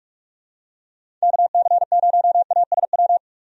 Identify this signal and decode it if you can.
Morse code, heard as a keyed tone, about 44 wpm, 700 Hz. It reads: KC0ASW
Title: USA Call Signs - 6 Characters - 45wpm